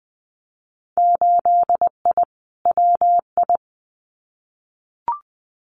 Morse code, heard as a keyed tone, about 20 wpm, 700 Hz. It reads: 8I WI
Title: Winter Field Day Exchanges 20wpm